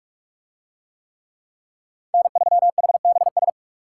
Morse code, heard as a keyed tone, about 44 wpm, 700 Hz. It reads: N3HBS